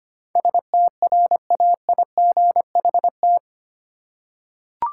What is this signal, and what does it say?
Morse code, heard as a keyed tone, about 25 wpm, 700 Hz. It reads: STRAIGHT
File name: Eight Letter Words - 25wpm